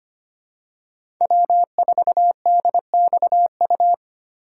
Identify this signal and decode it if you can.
Morse code, heard as a keyed tone, about 25 wpm, 700 Hz. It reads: W4DXU
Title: USA Call Signs - 5 Characters - 2x Character Spacing - 25wpm